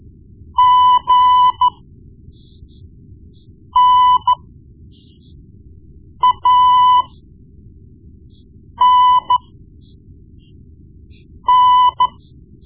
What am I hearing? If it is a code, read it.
Morse code, heard as a keyed tone, about 13 wpm, spaced as for 4 wpm, 1000 Hz. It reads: GNANN
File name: Morse Lessons / Morse Lessons / Tape-1B